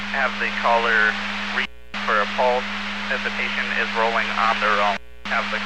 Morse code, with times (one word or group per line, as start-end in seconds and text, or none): none